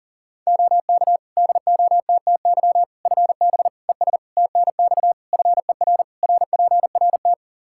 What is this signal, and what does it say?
Morse code, heard as a keyed tone, about 40 wpm, 700 Hz. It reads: OK DOTTY FB ES TNX FER RPRT
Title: K1USN SST QSO Practice 40wpm